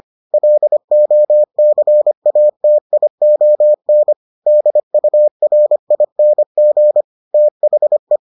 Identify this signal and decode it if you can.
Morse code, heard as a keyed tone, about 25 wpm, 600 Hz. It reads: LOCATION DURING THE